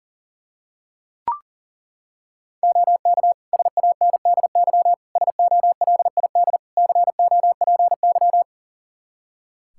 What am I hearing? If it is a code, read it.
Morse code, heard as a keyed tone, about 40 wpm, 700 Hz. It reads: OK SANDY SOLID COPY
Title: Practice QSO - Exchange 3 - More Info - 40wpm